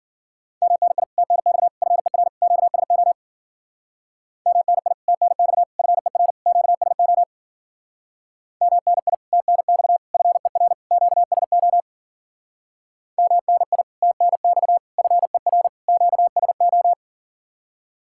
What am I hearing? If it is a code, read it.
Morse code, heard as a keyed tone, about 51 wpm, 700 Hz. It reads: MNI TNX FER QSO MNI TNX FER QSO MNI TNX FER QSO MNI TNX FER QSO